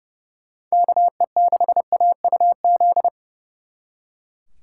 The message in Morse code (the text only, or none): KE6AUZ